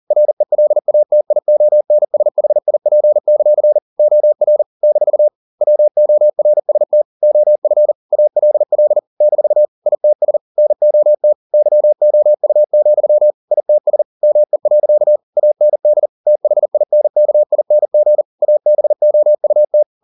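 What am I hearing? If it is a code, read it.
Morse code, heard as a keyed tone, about 40 wpm, 600 Hz. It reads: RELATIONSHIP; OR - WORST OF ALL - ITS NOT YOU, ITS ME. AND THINKING ABOUT
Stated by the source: Morse code at 40 wpm, practice text, CW med Gnister.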